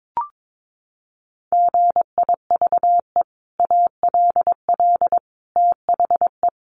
Morse code, heard as a keyed tone, about 22 wpm, 700 Hz. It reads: GIVE ALL THE
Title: Sentences from Top 600 Words 22wpm